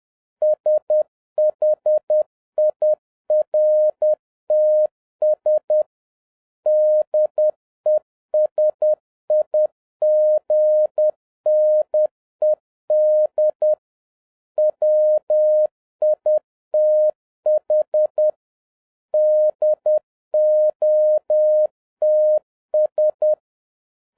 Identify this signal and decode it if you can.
Morse code, heard as a keyed tone, about 10 wpm, 610 Hz. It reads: SHIRTS DESIGNED WITH DOTS